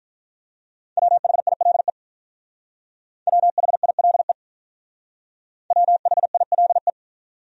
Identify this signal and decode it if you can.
Morse code, heard as a keyed tone, about 43 wpm, 700 Hz. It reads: WHILE WHILE WHILE
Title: Speed-Racing Top 500 Words 35wpm